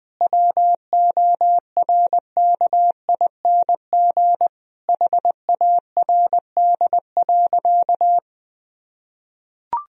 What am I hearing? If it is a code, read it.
Morse code, heard as a keyed tone, about 21 wpm, 700 Hz. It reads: WORKING HARD.
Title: Sentences from Top 1000 Words - Review Part 4 - 20wpm